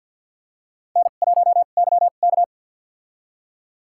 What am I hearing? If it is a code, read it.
Morse code, heard as a keyed tone, about 53 wpm, spaced as for 37 wpm, 700 Hz. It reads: N1YK
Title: USA Call Signs - 4 Characters - 2x Character Spacing - 50wpm